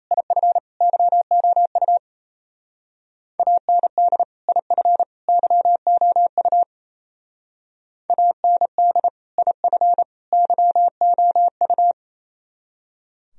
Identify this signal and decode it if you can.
Morse code, heard as a keyed tone, about 34 wpm, 700 Hz. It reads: IF YOU AND IF YOU AND IF YOU
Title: Speed-Racing - Sets of 3 Words 28wpm